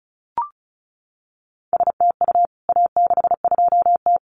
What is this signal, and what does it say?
Morse code, heard as a keyed tone, about 34 wpm, 700 Hz. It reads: STU A62T
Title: CWT Exchange 35wpm